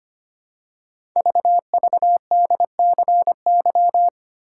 Morse code, heard as a keyed tone, about 25 wpm, 700 Hz. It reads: VVDCY
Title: Code Groups - 5 Letters - 25wpm